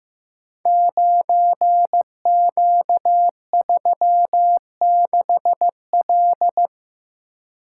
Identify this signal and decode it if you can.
Morse code, heard as a keyed tone, about 15 wpm, 700 Hz. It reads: 9Q36L
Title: Code Groups - 5 Letter:Numbers - 15wpm